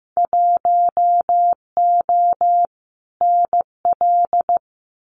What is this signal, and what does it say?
Morse code, heard as a keyed tone, about 15 wpm, 700 Hz. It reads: 1O NL
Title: Winter Field Day Exchanges Encore 15wpm